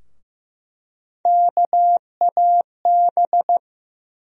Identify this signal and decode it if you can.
Morse code, heard as a keyed tone, about 15 wpm, 700 Hz. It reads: KAB
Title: USA Call Sign Postfixes - 3 Letters 15wpm